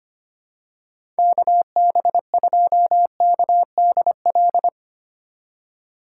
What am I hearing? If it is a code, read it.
Morse code, heard as a keyed tone, about 25 wpm, 700 Hz. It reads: KB2KDL
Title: USA Call Signs - 6 Characters - 4x Character Spacing - 25wpm